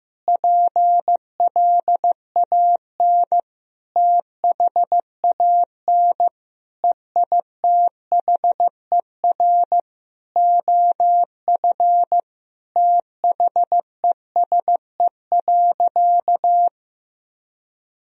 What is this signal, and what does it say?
Morse code, heard as a keyed tone, about 15 wpm, 700 Hz. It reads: PLAN THAN EITHER OF THESE.
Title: Sentences from Top 1000 Words - Review Part 4 - 15wpm